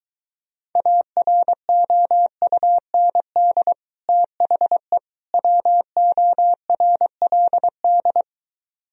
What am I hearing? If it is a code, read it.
Morse code, heard as a keyed tone, about 23 wpm, 700 Hz. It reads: AROUND THE WORLD